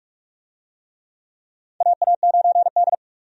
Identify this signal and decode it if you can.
Morse code, heard as a keyed tone, about 43 wpm, 700 Hz. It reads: AA9D
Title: USA Call Signs - 4 Characters - 45wpm